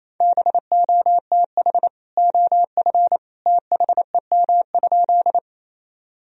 Morse code, heard as a keyed tone, about 28 wpm, 700 Hz. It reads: BOTH OF THEM?
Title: Sentences from Top 1000 Words - Review Part 1 - 28wpm